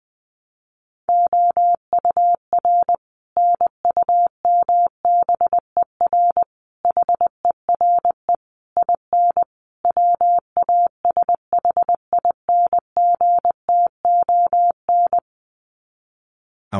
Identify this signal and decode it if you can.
Morse code, heard as a keyed tone, about 19 wpm, 700 Hz. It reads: OUR NUMBER HERE IN WASHINGTON
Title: Sets of 5 Words Encore2 20wpm